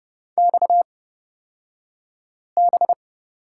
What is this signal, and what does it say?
Morse code, heard as a keyed tone, about 30 wpm, spaced as for 5 wpm, 700 Hz. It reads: XB
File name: Lesson 37d - Letter x - Two Characters 30wpm